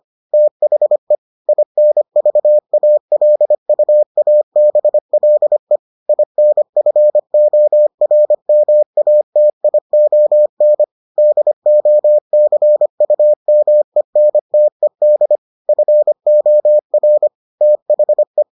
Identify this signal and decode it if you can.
Morse code, heard as a keyed tone, about 25 wpm, 600 Hz. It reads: THE INVALUABLE INFORMATION DOCUMENTED FOR THE